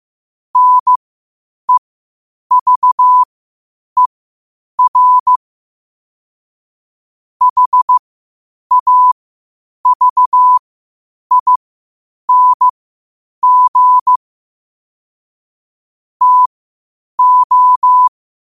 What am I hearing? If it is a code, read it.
Morse code, heard as a keyed tone, about 15 wpm, spaced as for 8 wpm, 1000 Hz. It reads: NEVER HAVING TO
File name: qotdfiles / 8wpm